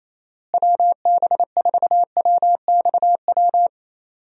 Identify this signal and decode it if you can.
Morse code, heard as a keyed tone, about 28 wpm, 700 Hz. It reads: WB4WXW
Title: USA Call Signs - 6 Characters - Repeated 2x - 28wpm